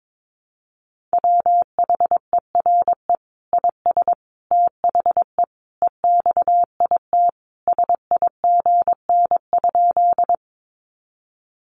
Morse code, heard as a keyed tone, about 22 wpm, 700 Hz. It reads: WHERE IS THE EXIT SIGN?
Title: Sentences from Top 1100 Words 22wpm